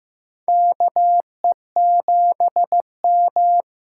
Morse code, heard as a keyed tone, about 15 wpm, 700 Hz. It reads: KE7M
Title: USA Call Signs - 6 Characters - 6x Character Spacing - 15wpm